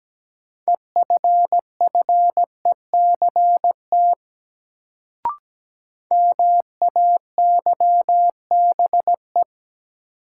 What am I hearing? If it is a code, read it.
Morse code, heard as a keyed tone, about 17 wpm, 700 Hz. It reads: EFFECT MAYBE
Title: Sentences from Top 1000 Words - New Word Drill - 17wpm